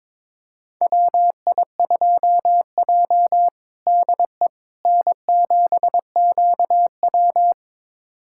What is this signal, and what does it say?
Morse code, heard as a keyed tone, about 22 wpm, 700 Hz. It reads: WI2J DE N7QW
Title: USA Callsign DE Callsign - 4 Characters - 22wpm